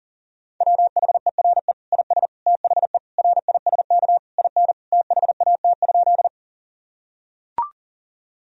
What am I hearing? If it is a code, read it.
Morse code, heard as a keyed tone, about 40 wpm, 700 Hz. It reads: WHERE IS THE RISK IN THAT?
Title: Sentences from Top 1000 Words - Review Part 4 - 40wpm